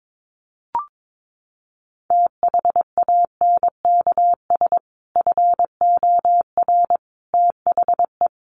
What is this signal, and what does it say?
Morse code, heard as a keyed tone, about 22 wpm, 700 Hz. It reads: THANKS FOR THE
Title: Sentences from Top 1500 Words 22wpm